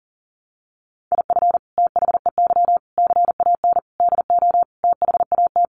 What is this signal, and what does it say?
Morse code, heard as a keyed tone, about 40 wpm, 700 Hz. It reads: IF THEY CAN DO THAT
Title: Sentences from Top 600 Words 40wpm